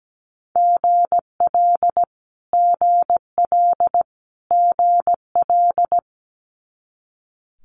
Morse code, heard as a keyed tone, about 17 wpm, 700 Hz. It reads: GL GL GL
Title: Multi-letter Phonemes Repeated 3x 17wpm